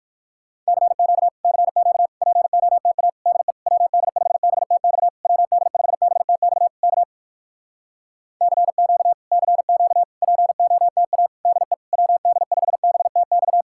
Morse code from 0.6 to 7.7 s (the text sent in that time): CQ CQ POTA DE WD5BTX WD5BTX K